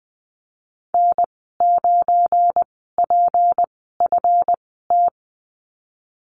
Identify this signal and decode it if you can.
Morse code, heard as a keyed tone, about 19 wpm, spaced as for 14 wpm, 700 Hz. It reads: N9PFT
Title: 173 Lesson 28g - Number 9 - Call Signs 20wpm